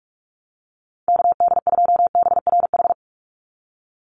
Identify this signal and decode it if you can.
Morse code, heard as a keyed tone, about 44 wpm, 700 Hz. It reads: KD2BRH
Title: USA Call Signs 45wpm